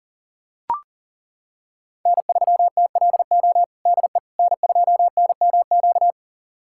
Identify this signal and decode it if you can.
Morse code, heard as a keyed tone, about 40 wpm, 700 Hz. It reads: N3TLO DE N2NMQ
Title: USA Callsign DE Callsign - 5 Characters - 40wpm